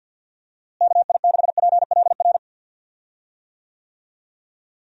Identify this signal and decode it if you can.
Morse code, heard as a keyed tone, about 53 wpm, 700 Hz. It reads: KI6PLR
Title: USA Call Signs - 6 Characters - Repeated 2x - 50wpm